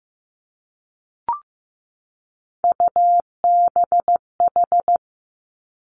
Morse code, heard as a keyed tone, about 15 wpm, 700 Hz. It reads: UBH
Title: Three Challenging Letter-Number Combinations 15wpm